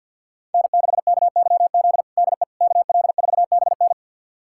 Morse code, heard as a keyed tone, about 47 wpm, 700 Hz. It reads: N6KYZ DE KL4BN
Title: USA Callsign DE Callsign - 5 Characters - 50wpm